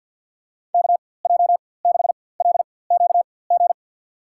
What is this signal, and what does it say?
Morse code, heard as a keyed tone, about 52 wpm, spaced as for 24 wpm, 700 Hz. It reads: KJ6LQG
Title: USA Call Signs - 6 Characters - 4x Character Spacing - 50wpm